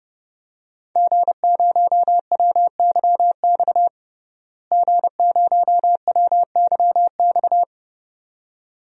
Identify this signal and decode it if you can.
Morse code, heard as a keyed tone, about 32 wpm, 700 Hz. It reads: G0WYX G0WYX
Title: DXCC-100 Call Signs - 5 Characters - Repeated 2x - 30wpm